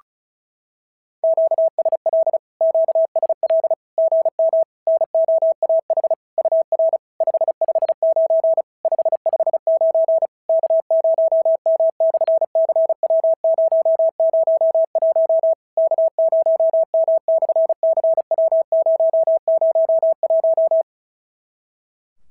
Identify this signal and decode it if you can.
Morse code, heard as a keyed tone, about 36 wpm, 650 Hz. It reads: QSL QSL GM NOAH UR 559 559 K0M/CW001 K0M/CW001
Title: SOTA QSO Practice 35wpm